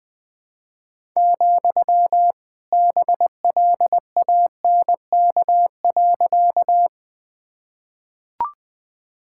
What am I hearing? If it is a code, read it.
Morse code, heard as a keyed tone, about 21 wpm, 700 Hz. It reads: , BLANK.
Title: Sentences with 2-Word Phrases - Spoken Beginning - 20wpm